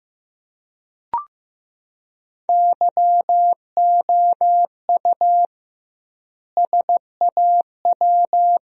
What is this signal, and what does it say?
Morse code, heard as a keyed tone, about 16 wpm, 700 Hz. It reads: YOU SAW
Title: 2x Word-Spacing - Sentences from Top 300 Words - 15wpm